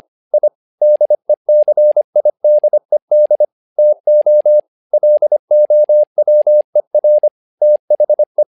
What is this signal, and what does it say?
Morse code, heard as a keyed tone, about 25 wpm, 600 Hz. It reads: I DECIDED TO LOWER THE
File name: learning center / tower trouble / tower trouble 25wpm